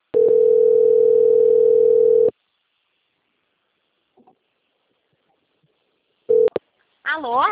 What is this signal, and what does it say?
Ringback tone (North America), 440 + 480 Hz, with speech.
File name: rg-1600-unknown- - - .15610